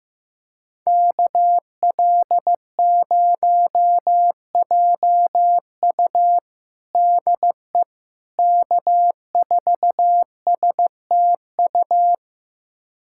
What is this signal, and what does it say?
Morse code, heard as a keyed tone, about 15 wpm, 700 Hz. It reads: KL0JU DE K4STU
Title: USA Callsign DE Callsign - 5 Characters - 15wpm